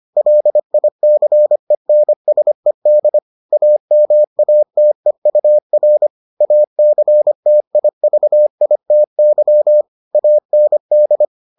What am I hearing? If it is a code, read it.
Morse code, heard as a keyed tone, about 25 wpm, 600 Hz. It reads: LICENSED AMATEUR ACTIVITY AND